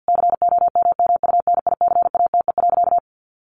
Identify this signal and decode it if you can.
Morse code, heard as a keyed tone, about 50 wpm, 700 Hz. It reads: COMMUNICATE.
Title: Sentences from Top 1300 Words 50wpm